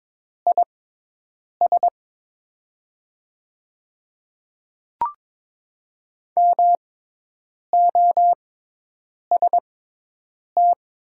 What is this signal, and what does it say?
Morse code, heard as a keyed tone, about 22 wpm, spaced as for 8 wpm, 700 Hz. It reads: IS MOST
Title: 6x Character Spacing - Top 100 Words - 22wpm